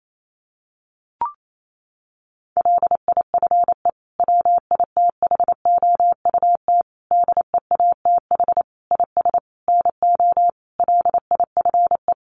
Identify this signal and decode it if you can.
Morse code, heard as a keyed tone, about 28 wpm, 700 Hz. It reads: LIFE WITHOUT DEATH IS NO LIFE
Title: Sentences from Top 1100 Words 28wpm